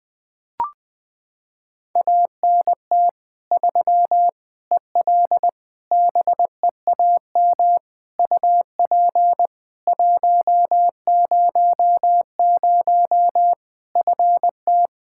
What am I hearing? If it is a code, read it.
Morse code, heard as a keyed tone, about 20 wpm, 700 Hz. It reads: ANT 3 EL BEAM UP 100 FT
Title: Practice QSO - Exchange 2 - Info - 20wpm